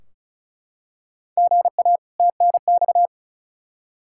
Morse code, heard as a keyed tone, about 36 wpm, 700 Hz. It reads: GA TNX
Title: K1USN SST QSO Practice 35wpm